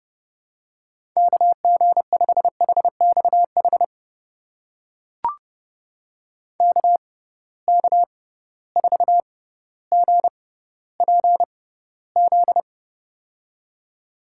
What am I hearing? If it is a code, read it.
Morse code, heard as a keyed tone, about 29 wpm, spaced as for 15 wpm, 700 Hz. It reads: KG5HXH KK4GPZ
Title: USA Call Signs - 6x Character Spacing - 30wpm